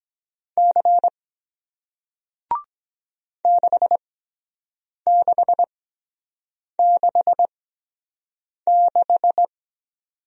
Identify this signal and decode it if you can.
Morse code, heard as a keyed tone, about 24 wpm, spaced as for 7 wpm, 700 Hz. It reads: C 6666